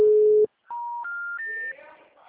Special information tone.